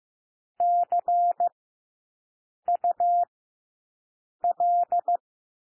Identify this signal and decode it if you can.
Morse code, heard as a keyed tone, about 15 wpm, 700 Hz. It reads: C U L
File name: cwmp3 / W1 SOUND WORDS 15 wpm